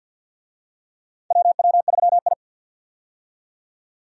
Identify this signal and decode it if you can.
Morse code, heard as a keyed tone, about 49 wpm, 700 Hz. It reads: WW3I
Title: USA Call Signs - 4 Characters - 50wpm